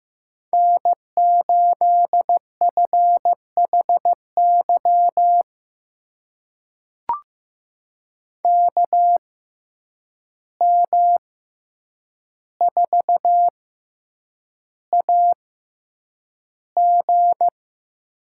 Morse code, heard as a keyed tone, about 15 wpm, spaced as for 8 wpm, 700 Hz. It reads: N8FHY KM4AG
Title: USA Call Signs - 5 Characters - 6x Character Spacing - 15wpm